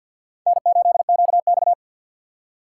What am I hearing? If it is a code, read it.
Morse code, heard as a keyed tone, about 53 wpm, 700 Hz. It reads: N8QX
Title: USA Call Signs - 4 Characters - 2x Character Spacing - 50wpm